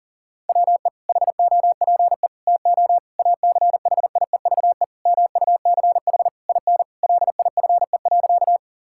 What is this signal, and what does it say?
Morse code, heard as a keyed tone, about 37 wpm, 700 Hz. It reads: WE HOPE TO ACHIEVE MUCH IN LIFE.
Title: Sentences from Top 1000 Words - Review Part 6 - 40wpm